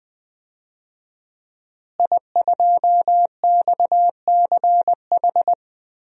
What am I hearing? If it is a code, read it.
Morse code, heard as a keyed tone, about 20 wpm, 700 Hz. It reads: I2XCH
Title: DXCC-100 Call Signs - 5 Characters - Repeated 3x - 20wpm